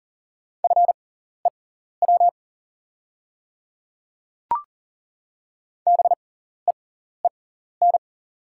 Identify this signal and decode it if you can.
Morse code, heard as a keyed tone, about 40 wpm, spaced as for 14 wpm, 690 Hz. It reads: FEW BEEN
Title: 6x Character Spacing - Top 300 Words - 40wpm